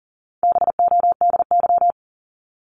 Morse code, heard as a keyed tone, about 40 wpm, 700 Hz. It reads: BODY
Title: Top 500 Words 40wpm